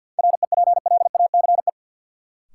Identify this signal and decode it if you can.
Morse code, heard as a keyed tone, about 53 wpm, 700 Hz. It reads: REPLACE
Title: Sentences from Top 1400 Words - New Word Drill - 50wpm